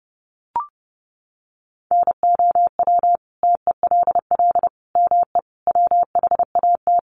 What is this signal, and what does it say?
Morse code, heard as a keyed tone, about 30 wpm, 700 Hz. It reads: NOW TELL ME WHAT
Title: Sentences from Top 1300 Words 30wpm